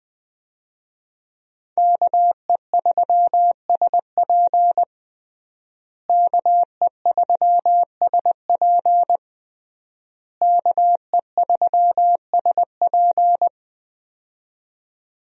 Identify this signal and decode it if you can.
Morse code, heard as a keyed tone, about 21 wpm, 700 Hz. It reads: KE3SP KE3SP KE3SP
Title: USA Call Signs - 5 Characters - Repeated 3x - 20wpm